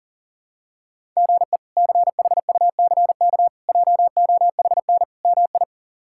Morse code, heard as a keyed tone, about 40 wpm, 700 Hz. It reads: GE CHUCK JOHN MI
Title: K1USN SST QSO Practice - 40wpm